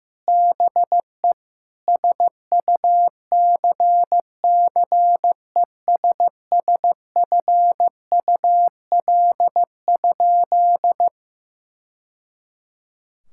Morse code, heard as a keyed tone, about 15 wpm, 700 Hz. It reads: BE SUCCESSFUL?
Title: Sentences from Top 1000 Words - Review Part 6 - 15wpm